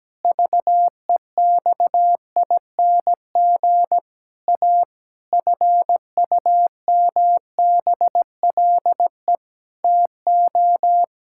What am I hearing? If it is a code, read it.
Morse code, heard as a keyed tone, about 17 wpm, 700 Hz. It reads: VEXING A FUMBLE TO
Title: Pangram Challenge v2 - 17wpm